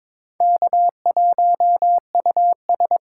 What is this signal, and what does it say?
Morse code, heard as a keyed tone, about 26 wpm, 700 Hz. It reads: K1US
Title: Speed Racing - USA 1x2 Call Signs 22wpm